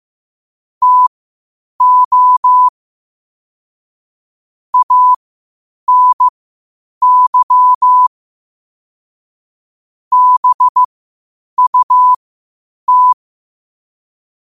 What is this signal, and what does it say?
Morse code, heard as a keyed tone, about 15 wpm, spaced as for 8 wpm, 1000 Hz. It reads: TO ANY BUT
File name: qotdfiles / 8wpm